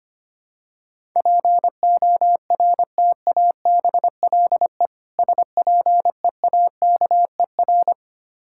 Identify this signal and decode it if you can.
Morse code, heard as a keyed tone, about 25 wpm, 700 Hz. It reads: PORTABLE SPEAKER
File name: practice / 25wpm / Difficult Words with Hints - 25wpm